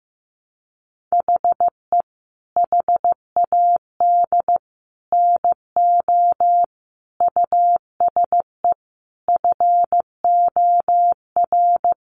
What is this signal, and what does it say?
Morse code, heard as a keyed tone, about 15 wpm, 700 Hz. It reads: HE HAD NO USE FOR
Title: Sentences from Top 1200 Words 15wpm